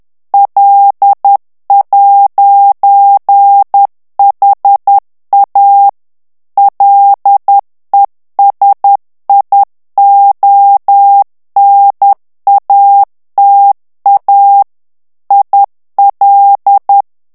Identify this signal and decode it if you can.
Morse code, heard as a keyed tone, about 10 wpm, 800 Hz. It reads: L'HA LESIONATA IL
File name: ITALIEN texte clair 1 vitesse 10 mots minute